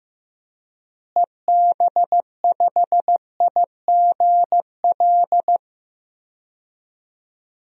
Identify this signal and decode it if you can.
Morse code, heard as a keyed tone, about 16 wpm, 700 Hz. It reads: EB5IGL
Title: DXCC-100 Call Signs - 6 Characters - Repeated 3x - 15wpm